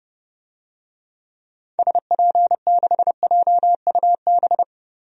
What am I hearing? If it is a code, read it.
Morse code, heard as a keyed tone, about 30 wpm, 690 Hz. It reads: SP6JUB
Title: DXCC-100 Call Signs - 6 Characters - 30wpm